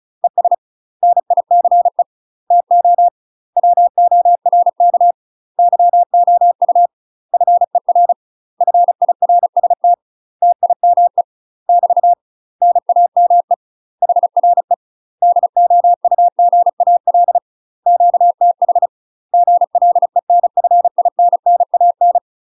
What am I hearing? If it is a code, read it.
Morse code, heard as a keyed tone, about 36 wpm, 700 Hz. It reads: ES NICE TO WORK YOU FER FIRST TIME <BT> NAME HRE DOUGAL QTH GLENFINNAN
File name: UK MASTER QSOs TRAINING / 35WPM / UK MASTER QSOs 0015